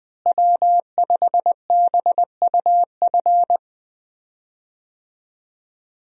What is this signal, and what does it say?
Morse code, heard as a keyed tone, about 20 wpm, 700 Hz. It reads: W5BUF
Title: USA Call Signs - 5 Characters - Repeated 3x - 20wpm